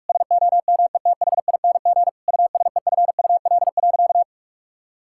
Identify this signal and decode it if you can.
Morse code, heard as a keyed tone, about 46 wpm, 700 Hz. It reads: SOMETHING USEFUL.
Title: Sentences from Top 1000 Words - Review Part 5 - 45wpm